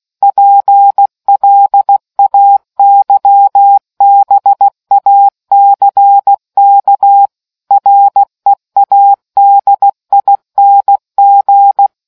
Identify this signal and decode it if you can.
Morse code, heard as a keyed tone, about 16 wpm, 800 Hz. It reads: PLAYBACK READING